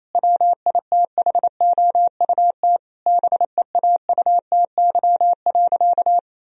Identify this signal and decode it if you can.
Morse code, heard as a keyed tone, about 28 wpm, 700 Hz. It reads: WITHOUT BEAUTY.
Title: Sentences from Top 1000 Words - Review Part 3 - 28wpm